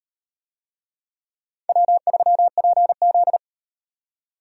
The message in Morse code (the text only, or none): W3PZ